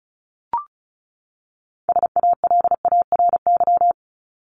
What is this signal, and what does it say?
Morse code, heard as a keyed tone, about 34 wpm, 700 Hz. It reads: SALARY
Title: Top 1751-2000 Words 35wpm